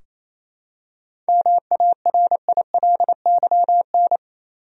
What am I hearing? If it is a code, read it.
Morse code, heard as a keyed tone, about 28 wpm, 700 Hz. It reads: MARILYN
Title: USA Common Names - 28wpm